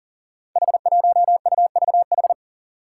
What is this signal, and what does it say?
Morse code, heard as a keyed tone, about 42 wpm, 700 Hz. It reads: H1UVH
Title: Five Challenging Letter-Number Combinations 40wpm